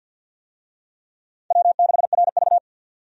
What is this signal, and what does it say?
Morse code, heard as a keyed tone, about 53 wpm, 700 Hz. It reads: W6RV
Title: USA Call Signs - 4 Characters - 50wpm